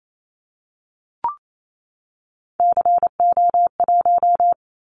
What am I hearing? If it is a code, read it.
Morse code, heard as a keyed tone, about 28 wpm, 700 Hz. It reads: CO1
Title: Callsign Prefixes for Top 100 DXCC - 28wpm